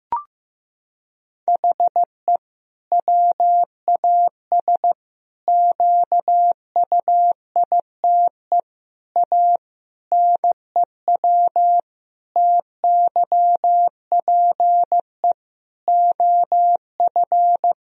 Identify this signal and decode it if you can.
Morse code, heard as a keyed tone, about 16 wpm, 700 Hz. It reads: HE WAS QUITE A NEW TYPE OF
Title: Sentences from Top 1000 Words - Review Part 3 - 15wpm